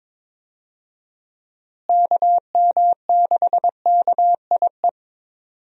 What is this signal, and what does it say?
Morse code, heard as a keyed tone, about 22 wpm, 700 Hz. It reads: KM6KIE
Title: USA Call Signs - 6 Characters - Repeated 3x - 22wpm